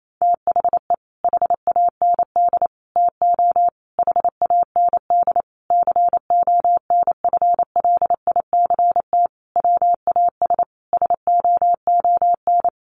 Morse code, heard as a keyed tone, about 28 wpm, 700 Hz. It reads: THE HAND TO HAND CONFLICT WAS SOON
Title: Sentences from Top 1300 Words 28wpm